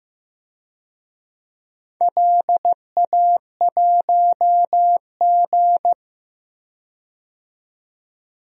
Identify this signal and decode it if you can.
Morse code, heard as a keyed tone, about 15 wpm, 700 Hz. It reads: LA1G
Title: DXCC-100 Call Signs - 4 Characters - 15wpm